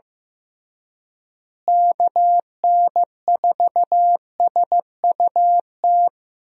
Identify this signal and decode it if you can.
Morse code, heard as a keyed tone, about 15 wpm, 700 Hz. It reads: KN4SUT
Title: USA Call Signs - 6 Characters - Repeated 2x - 15wpm